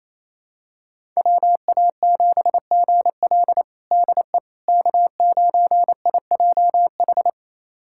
Morse code, heard as a keyed tone, about 28 wpm, 700 Hz. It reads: WA7GL DE K9IJH
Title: USA Callsign DE Callsign - 5 Characters - 28wpm